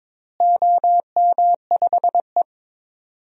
Morse code, heard as a keyed tone, about 22 wpm, 700 Hz. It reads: OM5E
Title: DXCC-100 Call Signs - 4 Characters - Repeated 2x - 22wpm